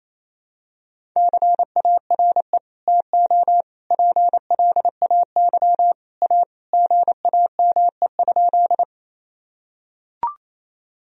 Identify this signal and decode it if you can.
Morse code, heard as a keyed tone, about 28 wpm, 700 Hz. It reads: CARE TO PLAY A GAME?
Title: Sentences from Top 1000 Words - Review Part 1 - 28wpm